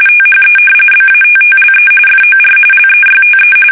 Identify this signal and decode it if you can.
RTTY, about 45.45 baud. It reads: THE BATTLE FOR LEYTE
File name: fsk850 / battleshipnj